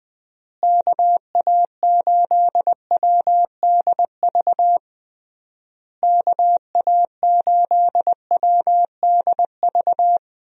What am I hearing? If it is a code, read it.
Morse code, heard as a keyed tone, about 20 wpm, 700 Hz. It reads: KA8WDV KA8WDV
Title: USA Call Signs - 6 Characters - Repeated 3x - 20wpm